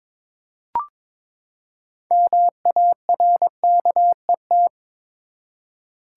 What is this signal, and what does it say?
Morse code, heard as a keyed tone, about 22 wpm, 700 Hz. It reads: MARKET
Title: Six Letter Words - 22wpm